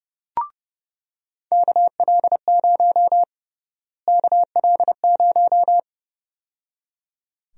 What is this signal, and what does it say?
Morse code, heard as a keyed tone, about 32 wpm, 700 Hz. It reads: KL0 KL0
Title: USA Call Sign Prefixes - Repeated 2x - 30wpm